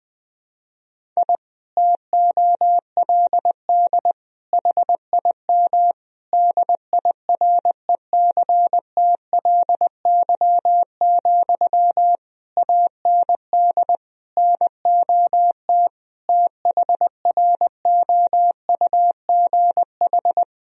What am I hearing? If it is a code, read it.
Morse code, heard as a keyed tone, about 20 wpm, 700 Hz. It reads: I TOLD HIM DIRECTLY, AND NOT THROUGH
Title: Sentences from Top 1000 Words - Review Part 1 - 20wpm